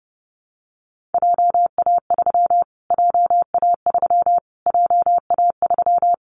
Morse code, heard as a keyed tone, about 29 wpm, 700 Hz. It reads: JA3 JA3 JA3
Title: Callsign Prefixes for Top 25 DXCC - Repeated 3x - 30wpm